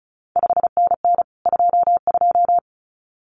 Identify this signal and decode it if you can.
Morse code, heard as a keyed tone, about 36 wpm, 700 Hz. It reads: 5NN 22
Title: CQ WW DX Contest Exchange Encore - 35wpm